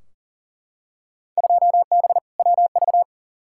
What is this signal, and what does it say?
Morse code, heard as a keyed tone, about 38 wpm, 700 Hz. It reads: 2B WV
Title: Field Day Exchanges Encore - 40wpm